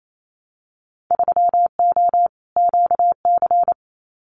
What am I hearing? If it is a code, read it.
Morse code, heard as a keyed tone, about 28 wpm, 700 Hz. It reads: 3O QC
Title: Winter Field Day Exchanges Encore 28wpm